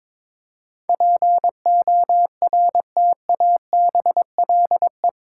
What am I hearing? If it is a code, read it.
Morse code, heard as a keyed tone, about 22 wpm, 690 Hz. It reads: PORTABLE
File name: Difficult Words with Hints - 22wpm